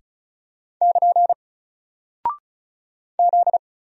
Morse code, heard as a keyed tone, about 36 wpm, 700 Hz. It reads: <KN> Z
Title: Single Character (Letter:Number:Punctuation:Prosign) 35wpm